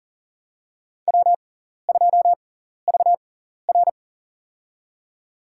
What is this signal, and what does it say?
Morse code, heard as a keyed tone, about 40 wpm, spaced as for 14 wpm, 700 Hz. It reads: W2VR
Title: USA Call Signs - 4 Characters - 6x Character Spacing - 40wpm